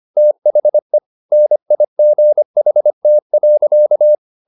Morse code, heard as a keyed tone, about 25 wpm, 600 Hz. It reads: THE NIGHT.